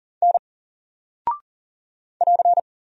Morse code, heard as a keyed tone, about 39 wpm, 700 Hz. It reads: N <AR>
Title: Single Character (Letter:Number:Punctuation:Prosign) 40wpm